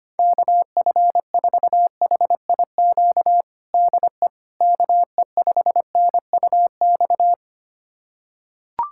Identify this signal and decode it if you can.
Morse code, heard as a keyed tone, about 25 wpm, 700 Hz. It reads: KF4HIQ DE KE5NUX
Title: USA Callsign DE Callsign 25wpm